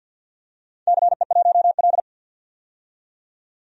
Morse code, heard as a keyed tone, about 47 wpm, 700 Hz. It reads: CE1L
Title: DXCC-100 Call Signs - 4 Characters - 50wpm